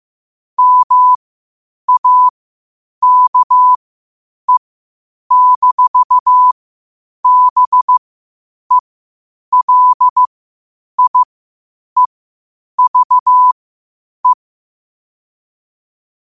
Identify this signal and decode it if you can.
Morse code, heard as a keyed tone, about 15 wpm, spaced as for 8 wpm, 1000 Hz. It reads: MAKE-BELIEVE
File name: qotdfiles / 8wpm